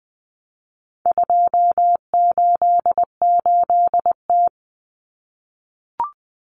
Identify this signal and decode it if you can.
Morse code, heard as a keyed tone, about 20 wpm, 700 Hz. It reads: 288T
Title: Common Cut Numbers - 4-Digit 20wpm